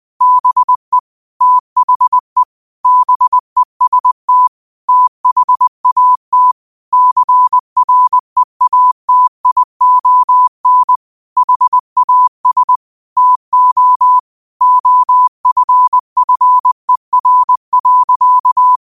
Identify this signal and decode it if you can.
Morse code, heard as a keyed tone, about 20 wpm, 1000 Hz. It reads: BE THE BEST THAT CREATION HAS TO OFFER.
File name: qotdfiles / 20wpm